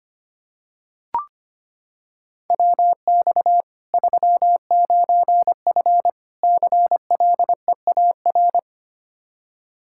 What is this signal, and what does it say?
Morse code, heard as a keyed tone, about 25 wpm, 700 Hz. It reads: WX 39F CLEAR
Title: Practice QSO - Exchange 2 - Info - 25wpm